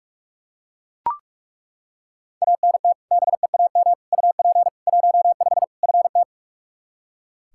Morse code, heard as a keyed tone, about 41 wpm, 700 Hz. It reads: ANT BEAM UP 15 FT
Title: Practice QSO - Exchange 2 - Info - 45wpm